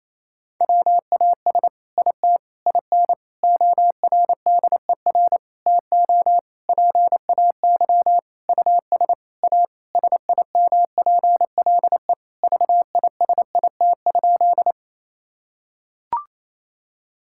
Morse code, heard as a keyed tone, about 28 wpm, 700 Hz. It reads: WAS IT IN ORDER TO PAY US A SIMPLE VISIT?
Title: Sentences from Top 1000 Words - Review Part 6 - 28wpm